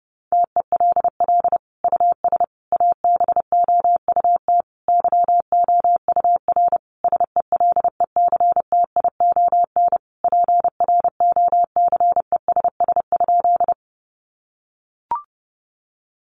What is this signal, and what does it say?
Morse code, heard as a keyed tone, about 30 wpm, 700 Hz. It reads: TELL US ABOUT YOUR SELECTION PROCESS?
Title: Sentences from Top 1400 Words 30wpm